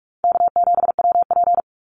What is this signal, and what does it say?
Morse code, heard as a keyed tone, about 44 wpm, 700 Hz. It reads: K7WP
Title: Easy Call Signs Repeated 2x - 45wpm